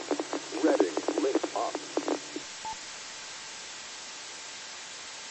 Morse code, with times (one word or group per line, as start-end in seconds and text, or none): none